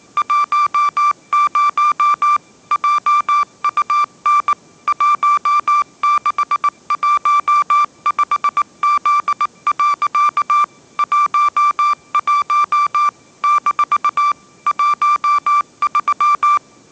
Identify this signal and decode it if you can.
Morse code, heard as a keyed tone, about 19 wpm, 1180 Hz. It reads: 10 JUN 1615Z. 11 - 13
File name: mrx083 20wpm